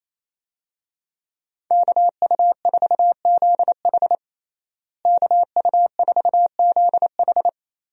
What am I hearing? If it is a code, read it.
Morse code, heard as a keyed tone, about 28 wpm, 700 Hz. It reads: KU4ZH KU4ZH
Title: USA Call Signs - 5 Characters - Repeated 3x - 28wpm